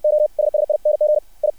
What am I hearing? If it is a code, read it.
Morse code, heard as a keyed tone, about 60 wpm, 600 Hz. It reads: OSSING I
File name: archived / CRSnet